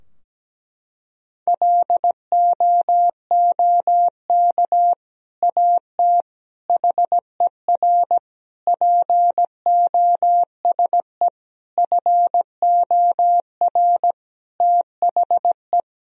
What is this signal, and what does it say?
Morse code, heard as a keyed tone, about 17 wpm, 700 Hz. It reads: LOOK AT HER POSE FOR THE
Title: Sentences from Top 1000 Words - Review Part 2 - 17wpm